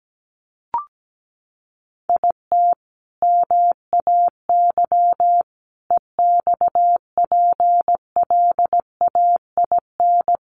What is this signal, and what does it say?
Morse code, heard as a keyed tone, about 17 wpm, 700 Hz. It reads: IT MAY EXPLAIN
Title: Sentences from Top 1100 Words 17wpm